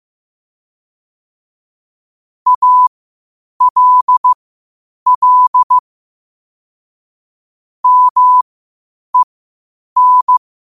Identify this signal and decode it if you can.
Morse code, heard as a keyed tone, about 15 wpm, spaced as for 8 wpm, 1000 Hz. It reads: ALL MEN
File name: qotdfiles / 8wpm